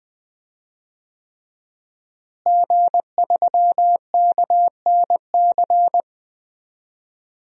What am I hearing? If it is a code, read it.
Morse code, heard as a keyed tone, about 20 wpm, 700 Hz. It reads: G3KNC
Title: DXCC-100 Call Signs - 5 Characters - Repeated 3x - 20wpm